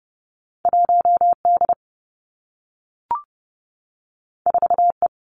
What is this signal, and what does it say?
Morse code, heard as a keyed tone, about 30 wpm, 700 Hz. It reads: 1 D 4 E
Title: Two Challenging Letter-Number Combinations 30wpm